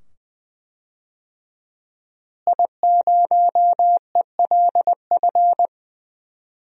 Morse code, heard as a keyed tone, about 20 wpm, 700 Hz. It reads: I0ELF
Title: DXCC-100 Call Signs - 5 Characters - Repeated 2x - 20wpm